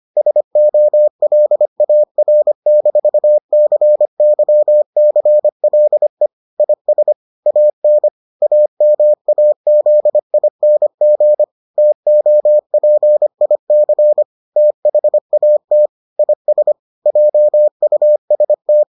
Morse code, heard as a keyed tone, about 25 wpm, 600 Hz. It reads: SOLAR-CYCLE IS AN AMAZING TOPIC THAT IS JUST